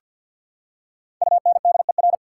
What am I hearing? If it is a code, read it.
Morse code, heard as a keyed tone, about 48 wpm, 700 Hz. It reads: UNDER
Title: Prefixes and Suffixes 50wpm